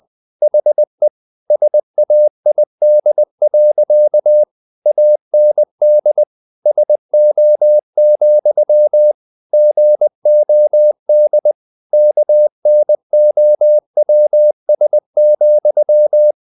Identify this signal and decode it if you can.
Morse code, heard as a keyed tone, about 20 wpm, 600 Hz. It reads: HE SAID. AND SO, GOD KNOWS,
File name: the old man and the sea / the old man and the sea part 1 20wpm